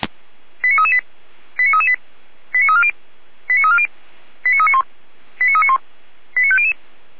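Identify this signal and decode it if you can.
Five-tone selective call (ZVEI), heard as quick runs of five tones, tones of about 70 ms. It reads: 89338 89338 89340 89340 89391 89391 89500
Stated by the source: ZVEI five-tone call received by SDR, from the Feuerwehr Orlamünde audio server.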